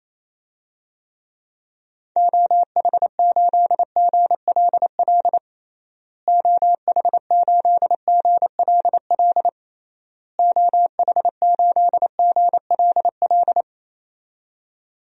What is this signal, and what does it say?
Morse code, heard as a keyed tone, about 28 wpm, 700 Hz. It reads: OH8GLL OH8GLL OH8GLL
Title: DXCC-100 Call Signs - 6 Characters - Repeated 3x - 28wpm